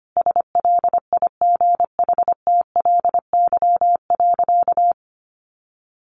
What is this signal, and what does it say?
Morse code, heard as a keyed tone, about 25 wpm, 700 Hz. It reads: SLIGHTLY.
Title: Sentences from Top 1300 Words 25wpm